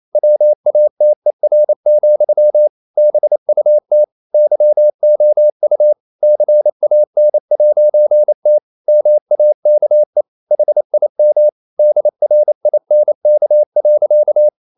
Morse code, heard as a keyed tone, about 28 wpm, 600 Hz. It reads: WATER, BUT YOU CAN'T MAKE HIM DRINK.
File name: CW / Practice / EnglishSayings28